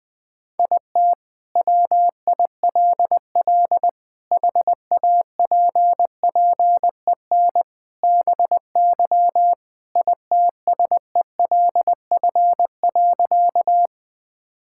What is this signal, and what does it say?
Morse code, heard as a keyed tone, about 21 wpm, 700 Hz. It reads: IT WILL HAPPEN BY ITSELF.
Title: Sentences from Top 1000 Words - Review Part 5 - 20wpm